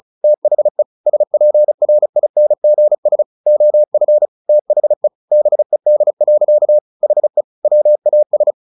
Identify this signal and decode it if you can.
Morse code, heard as a keyed tone, about 34 wpm, 600 Hz. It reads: THE SPRINGS OF THE BED. HE WAS
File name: the old man and the sea / the old man and the sea 1 35wpm